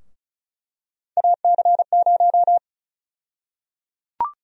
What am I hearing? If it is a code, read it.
Morse code, heard as a keyed tone, about 36 wpm, 700 Hz. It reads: AC0
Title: USA Call Sign Prefixes - 35wpm